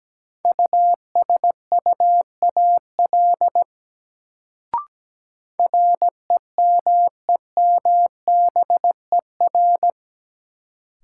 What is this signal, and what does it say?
Morse code, heard as a keyed tone, about 17 wpm, 700 Hz. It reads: USUAL REMEMBER
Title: Sentences from Top 400 Words - New Word Drill - 17wpm